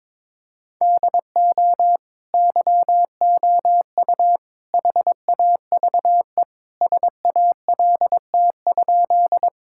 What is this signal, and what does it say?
Morse code, heard as a keyed tone, about 22 wpm, 700 Hz. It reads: DO YOU HAVE SALT?
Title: Sentences from Top 1000 Words - Review Part 5 - 22wpm